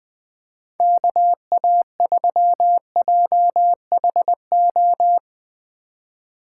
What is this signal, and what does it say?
Morse code, heard as a keyed tone, about 21 wpm, 700 Hz. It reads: KA3JHO
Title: USA Call Signs - 6 Characters - 4x Character Spacing - 20wpm